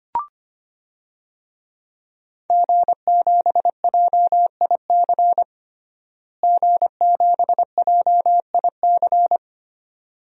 Morse code, heard as a keyed tone, about 25 wpm, 700 Hz. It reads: G7JIC G7JIC
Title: DXCC-100 Call Signs - 5 Characters - Repeated 2x - 25wpm